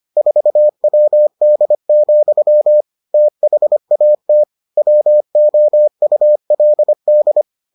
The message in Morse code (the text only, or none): 4WD, THAT WOULD